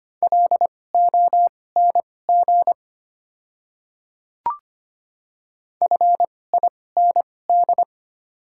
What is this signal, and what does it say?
Morse code, heard as a keyed tone, about 25 wpm, spaced as for 18 wpm, 700 Hz. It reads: LONG FIND
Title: 2x Character Spacing - Top 300 Words - 25wpm